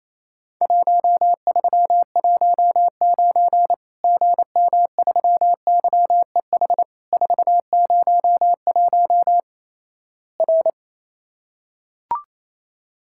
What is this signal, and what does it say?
Morse code, heard as a keyed tone, about 28 wpm, 700 Hz. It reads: 1319 GM3YEH 401 R
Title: Worked All Europe DX Contest - QTC Exchange - 28wpm